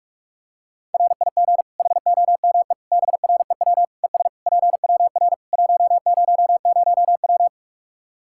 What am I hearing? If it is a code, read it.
Morse code, heard as a keyed tone, about 46 wpm, 700 Hz. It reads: RIG HOME BREW ES PWR 100W
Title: Practice QSO - Exchange 2 - Info - 45wpm